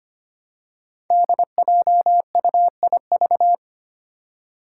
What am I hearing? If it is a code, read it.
Morse code, heard as a keyed tone, about 25 wpm, 700 Hz. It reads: DJUIV
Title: Five Challenging Letter-Number Combinations 25wpm